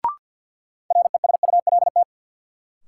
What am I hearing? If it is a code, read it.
Morse code, heard as a keyed tone, about 52 wpm, 700 Hz. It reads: RESULT
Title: Sentences from Top 700 Words - New Word Drill - 50wpm